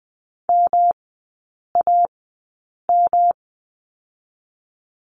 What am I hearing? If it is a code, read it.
Morse code, heard as a keyed tone, about 20 wpm, spaced as for 8 wpm, 700 Hz. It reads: MAM
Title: 107 Lesson 18e - Letter m - Three Characters 20wpm